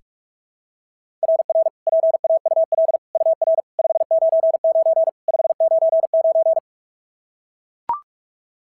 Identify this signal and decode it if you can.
Morse code, heard as a keyed tone, about 44 wpm, 690 Hz. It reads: RR PAUL UR 599 599
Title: SOTA QSO Practice 45wpm